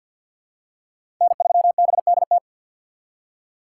Morse code, heard as a keyed tone, about 52 wpm, 700 Hz. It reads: N3BDT
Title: USA Call Signs - 5 Characters - 4x Character Spacing - 50wpm